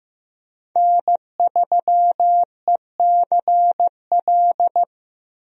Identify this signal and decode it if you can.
Morse code, heard as a keyed tone, about 15 wpm, 700 Hz. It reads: N3ECL
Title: USA Call Signs - 5 Characters - Repeated 3x - 15wpm